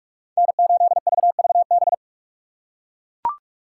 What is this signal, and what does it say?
Morse code, heard as a keyed tone, about 42 wpm, 700 Hz. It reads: N8VVB